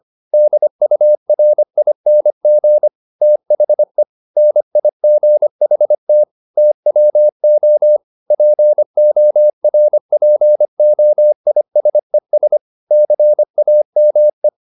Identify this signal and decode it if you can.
Morse code, heard as a keyed tone, about 25 wpm, 600 Hz. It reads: DURING THE NIGHT TWO PORPOISES CAME
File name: the old man and the sea part 1 25wpm